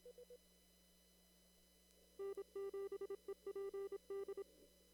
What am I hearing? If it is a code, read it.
Morse code, heard as a keyed tone, about 26 wpm, 400 Hz. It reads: N7EPD
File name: CW / kt5j wpx cw-op-k5pi